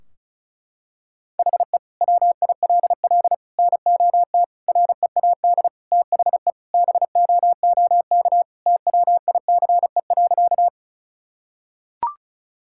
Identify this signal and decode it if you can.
Morse code, heard as a keyed tone, about 34 wpm, 700 Hz. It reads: HE WILL NOT READ THE BOOK TWICE.
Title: Sentences from Top 1600 Words 35wpm